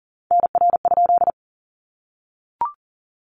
Morse code, heard as a keyed tone, about 39 wpm, 700 Hz. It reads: NR?
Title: Common QSO Phrases 40wpm